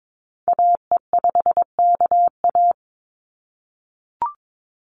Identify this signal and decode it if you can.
Morse code, heard as a keyed tone, about 22 wpm, 700 Hz. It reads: AE5KA
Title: NA SOTA Activator Callsigns 22wpm